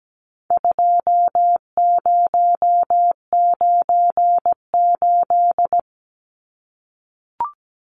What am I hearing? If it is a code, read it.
Morse code, heard as a keyed tone, about 17 wpm, 700 Hz. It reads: 2098
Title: Numbers - 4-Digit 17wpm